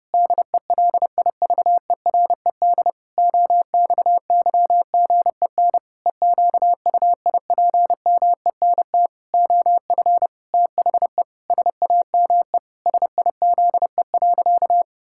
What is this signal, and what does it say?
Morse code, heard as a keyed tone, about 30 wpm, 700 Hz. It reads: DELIVERED OXYGEN EQUIPMENT OF THE SAME SIZE.
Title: Pangram Challenge v2 - 30wpm